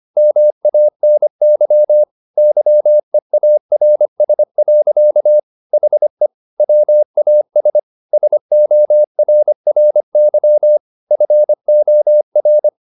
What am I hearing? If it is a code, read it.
Morse code, heard as a keyed tone, about 25 wpm, 600 Hz. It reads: MANY YEARS. HE WAS SORRY FOR
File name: the old man and the sea part 1 25wpm